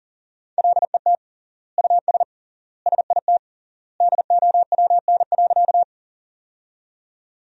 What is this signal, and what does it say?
Morse code, heard as a keyed tone, about 40 wpm, 700 Hz. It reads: LET US SIT DOWN.
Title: 3x Word-Spacing - Sentences from Top 500 Words - 40wpm